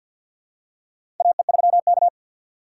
Morse code, heard as a keyed tone, about 51 wpm, 700 Hz. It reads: AE3K
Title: USA Call Signs - 4x Character Spacing - 50wpm